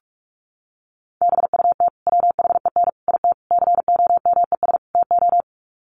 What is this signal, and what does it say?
Morse code, heard as a keyed tone, about 44 wpm, 700 Hz. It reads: BUT WHEN IT COMES TO
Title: Sets of 5 Words 45wpm